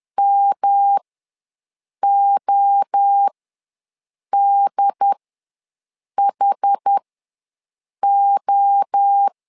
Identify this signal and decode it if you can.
Morse code, heard as a keyed tone, about 10 wpm, spaced as for 6 wpm, 800 Hz. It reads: MODHO
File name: LEZIONI / 3 MWRO / 3 MWRO ws 10wpm 2310